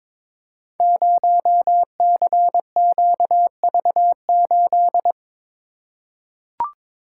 Morse code, heard as a keyed tone, about 22 wpm, 700 Hz. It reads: 0CQV8
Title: Code Groups - 5 Letter:Numbers - 22wpm